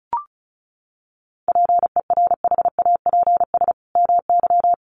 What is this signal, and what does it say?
Morse code, heard as a keyed tone, about 34 wpm, 700 Hz. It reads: PERHAPS MY
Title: Sentences from Top 1100 Words 35wpm